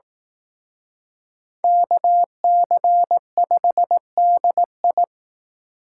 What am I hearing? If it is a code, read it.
Morse code, heard as a keyed tone, about 18 wpm, 700 Hz. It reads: KC5DI
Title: Speed Racing - USA 2x2 Call Signs 15wpm